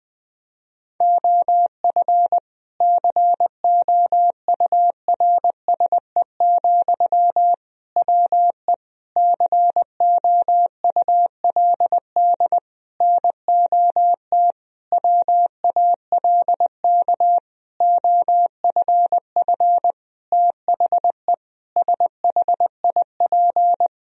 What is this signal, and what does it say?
Morse code, heard as a keyed tone, about 20 wpm, 700 Hz. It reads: OF COURSE, WE COULD NOT WALK OFF THE SHIP
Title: Sentences from Top 1000 Words - Review Part 2 - 20wpm